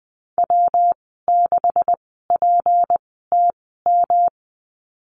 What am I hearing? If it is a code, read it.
Morse code, heard as a keyed tone, about 19 wpm, spaced as for 14 wpm, 700 Hz. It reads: W6PTM
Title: 238 Lesson 38g - Letter q - Call Signs 20wpm